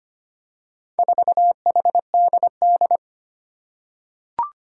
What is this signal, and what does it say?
Morse code, heard as a keyed tone, about 25 wpm, 700 Hz. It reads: 4HDD
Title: Four Challenging Letter-Number Combinations 25wpm